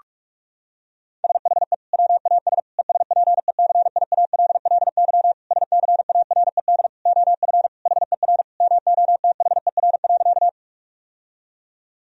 Morse code, heard as a keyed tone, about 46 wpm, 700 Hz. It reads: SHE WAS ESPECIALLY SCARED OF HER MOTHER.
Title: Sentences from Top 1000 Words - Review Part 4 - 45wpm